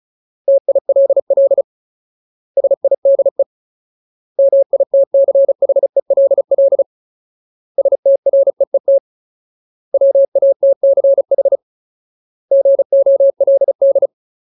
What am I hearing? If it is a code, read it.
Morse code, heard as a keyed tone, about 36 wpm, 550 Hz. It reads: TILL SIDE MITCHELL STREET WATCH GOLD
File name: Ultimate Word List / 35WPM Ultimate